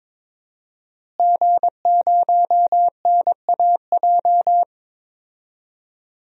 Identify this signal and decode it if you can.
Morse code, heard as a keyed tone, about 22 wpm, 700 Hz. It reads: G0NAJ
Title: DXCC-100 Call Signs - 5 Characters - Repeated 2x - 22wpm